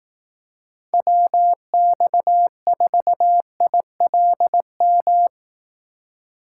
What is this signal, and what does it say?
Morse code, heard as a keyed tone, about 18 wpm, 700 Hz. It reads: WX4ILM
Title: Speed Racing - USA Call Signs 15wpm